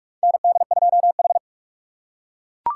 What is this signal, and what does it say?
Morse code, heard as a keyed tone, about 41 wpm, 700 Hz. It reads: ND2H
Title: USA Call Signs - 3x Character Spacing - 45wpm